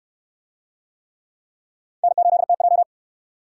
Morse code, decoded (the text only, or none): N8TJ